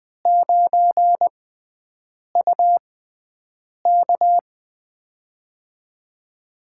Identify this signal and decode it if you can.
Morse code, heard as a keyed tone, about 20 wpm, spaced as for 7 wpm, 700 Hz. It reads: 9UK
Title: USA Call Signs - 5 Characters - 6x Character Spacing - 20wpm